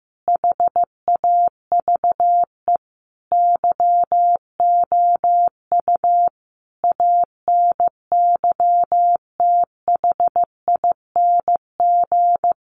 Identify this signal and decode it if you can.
Morse code, heard as a keyed tone, about 15 wpm, 700 Hz. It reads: HAVE YOU ANYTHING
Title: Sentences from Top 1000 Words 15wpm